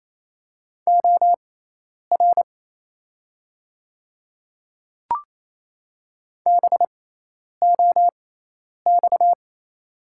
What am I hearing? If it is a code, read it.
Morse code, heard as a keyed tone, about 28 wpm, spaced as for 10 wpm, 700 Hz. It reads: OR BOX
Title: 6x Character Spacing - Top 500 Words - 28wpm